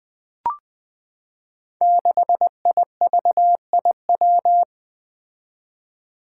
Morse code, heard as a keyed tone, about 20 wpm, 700 Hz. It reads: 6IVIW
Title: Five Challenging Letter-Number Combinations 20wpm